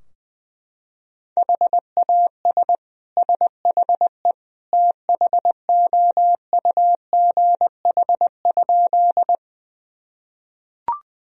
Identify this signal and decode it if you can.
Morse code, heard as a keyed tone, about 21 wpm, 700 Hz. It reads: HAS SHE THOUGH?
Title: Sentences from Top 1000 Words - Review Part 2 - 20wpm